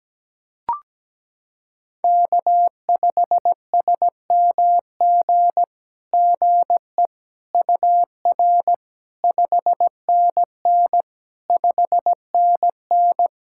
Morse code, heard as a keyed tone, about 17 wpm, 700 Hz. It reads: K5SMG GE UR 5NN 5NN
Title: SOTA QSO Practice 17wpm